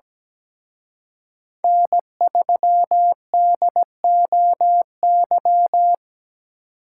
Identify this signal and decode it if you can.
Morse code, heard as a keyed tone, about 20 wpm, 700 Hz. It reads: N3DOY
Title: Speed Racing - USA Call Signs 17wpm